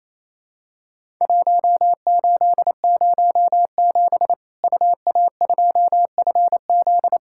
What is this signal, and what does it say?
Morse code, heard as a keyed tone, about 28 wpm, 700 Hz. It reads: 1807 UA2FZ
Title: Worked All Europe DX Contest - QTC Exchange - 28wpm